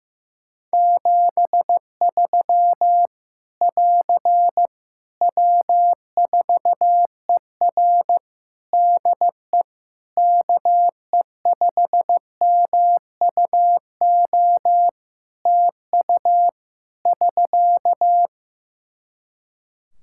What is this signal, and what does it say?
Morse code, heard as a keyed tone, about 14 wpm, 700 Hz. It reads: 73 <AR> W4ER DE KE5MUO TU <SK>
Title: K1USN SST QSO Practice 15wpm